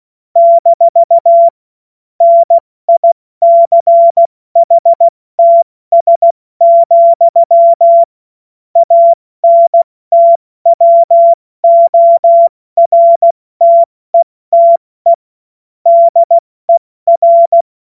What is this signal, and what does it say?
Morse code, heard as a keyed tone, about 16 wpm, 680 Hz. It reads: - NICHTS, ANTWORTETE DER
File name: Morsekurs / Witze / Witz 10. Mai 2022 70 F80